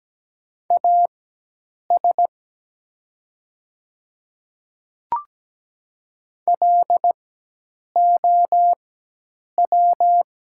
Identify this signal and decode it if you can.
Morse code, heard as a keyed tone, about 17 wpm, spaced as for 8 wpm, 700 Hz. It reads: AS LOW
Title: 4x Character Spacing - Top 200 Words - 17wpm